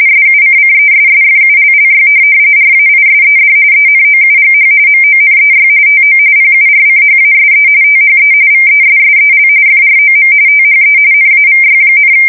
RTTY, about 45.45 baud. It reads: SUCCESSFULLY RAISED ENOUGH MONEY TO PURCHASE THE VESSEL FROM THE NAVY,